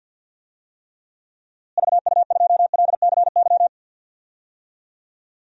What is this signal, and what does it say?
Morse code, heard as a keyed tone, about 50 wpm, 700 Hz. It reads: VU2LCY
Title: DXCC-100 Call Signs - 6 Characters - Repeated 2x - 50wpm